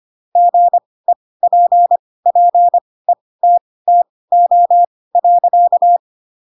Morse code, heard as a keyed tone, about 25 wpm, spaced as for 18 wpm, 700 Hz. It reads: GEPPETTO.
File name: Pinocchio ch3 12 25 ik1hge